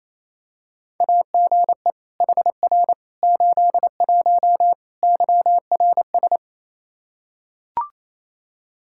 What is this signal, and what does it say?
Morse code, heard as a keyed tone, about 28 wpm, 700 Hz. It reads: AGE HR 81 YRS
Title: Practice QSO - Exchange 3 - More Info - 28wpm